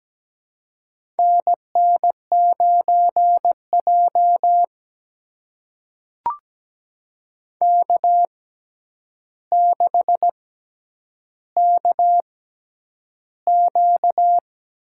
Morse code, heard as a keyed tone, about 17 wpm, spaced as for 9 wpm, 700 Hz. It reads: NN9J K6KQ
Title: USA Call Signs - 4 Characters - 6x Character Spacing - 17wpm